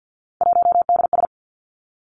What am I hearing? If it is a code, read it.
Morse code, heard as a keyed tone, about 50 wpm, 700 Hz. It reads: 1DS
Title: Three Challenging Letter-Number Combinations 50wpm